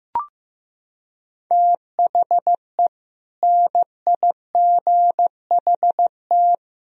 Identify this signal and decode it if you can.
Morse code, heard as a keyed tone, about 16 wpm, 700 Hz. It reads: THE NIGHT
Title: Sentences from Top 500 Words and Abbreviations 15wpm